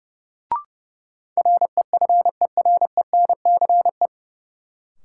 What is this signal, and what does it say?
Morse code, heard as a keyed tone, about 30 wpm, 700 Hz. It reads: REFERENCE
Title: Sentences from Top 1200 Words - New Word Drill - 30wpm